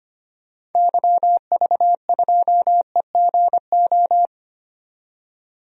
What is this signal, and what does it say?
Morse code, heard as a keyed tone, about 25 wpm, 700 Hz. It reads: YV2EGO
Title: DXCC-100 Call Signs - 6 Characters - 25wpm